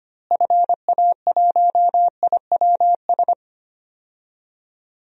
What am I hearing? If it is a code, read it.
Morse code, heard as a keyed tone, about 25 wpm, 700 Hz. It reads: FA1IWS
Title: DXCC-100 Call Signs - 6 Characters - Repeated 3x - 25wpm